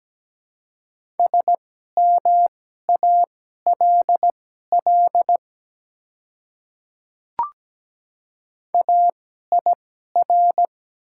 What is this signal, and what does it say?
Morse code, heard as a keyed tone, about 17 wpm, spaced as for 12 wpm, 700 Hz. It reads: SMALL AIR
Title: 2x Character Spacing - Top 200 Words - 17wpm